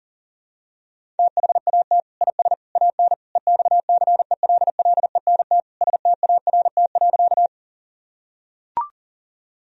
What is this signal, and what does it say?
Morse code, heard as a keyed tone, about 40 wpm, 700 Hz. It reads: THAT IS AN EXCELLENT START.
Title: Sentences from Top 1000 Words - Review Part 1 - 40wpm